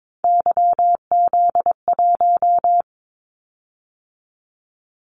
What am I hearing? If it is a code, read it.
Morse code, heard as a keyed tone, about 22 wpm, 700 Hz. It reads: YZ1
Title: Callsign Prefixes for Top 25 DXCC - 22wpm